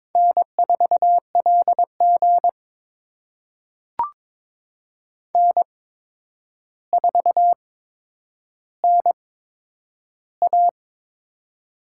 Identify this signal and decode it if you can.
Morse code, heard as a keyed tone, about 22 wpm, spaced as for 9 wpm, 700 Hz. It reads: N4LG N4NA
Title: USA Call Signs - 4 Characters - 8x Character Spacing - 22wpm